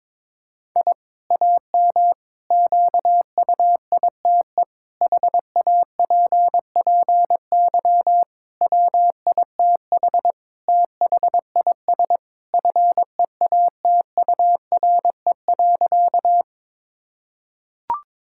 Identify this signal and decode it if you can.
Morse code, heard as a keyed tone, about 22 wpm, 700 Hz. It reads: I AM QUITE HAPPY WITH THIS FEATURE.
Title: Sentences from Top 1000 Words - Review Part 5 - 22wpm